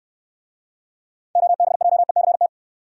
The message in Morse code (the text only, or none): OBJECT